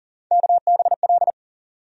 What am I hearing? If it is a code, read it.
Morse code, heard as a keyed tone, about 40 wpm, 700 Hz. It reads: KBL
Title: USA Call Sign Postfixes - 3 Letters 40wpm